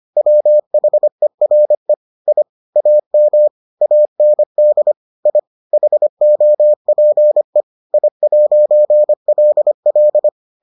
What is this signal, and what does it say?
Morse code, heard as a keyed tone, about 25 wpm, 600 Hz. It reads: WHERE I AM AND I HOPE I'LL